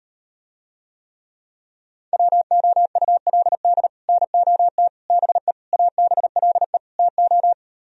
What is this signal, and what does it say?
Morse code, heard as a keyed tone, about 38 wpm, 700 Hz. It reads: WOULD NOT BE ABLE TO